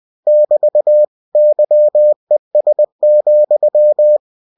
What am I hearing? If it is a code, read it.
Morse code, heard as a keyed tone, about 20 wpm, 600 Hz. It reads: <BT> YES,
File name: cw-books / winnie-the-pooh / winnie-20wpm-page-0105